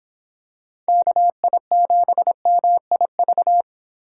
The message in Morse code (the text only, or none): KI7MIV